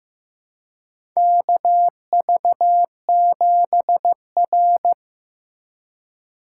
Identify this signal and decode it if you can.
Morse code, heard as a keyed tone, about 15 wpm, 700 Hz. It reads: KV7R
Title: USA Call Signs - 4 Characters - 4x Character Spacing - 15wpm